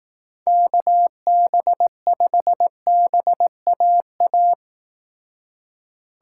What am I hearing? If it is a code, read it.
Morse code, heard as a keyed tone, about 18 wpm, 700 Hz. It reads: KB5BAA
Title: Speed Racing - USA 2x3 Call Signs 15wpm